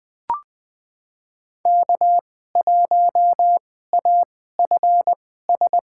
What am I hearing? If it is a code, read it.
Morse code, heard as a keyed tone, about 20 wpm, spaced as for 15 wpm, 700 Hz. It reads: K1AFS
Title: USA Call Signs - 5 Characters - 2x Character Spacing - 20wpm